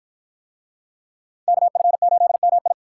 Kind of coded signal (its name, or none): Morse code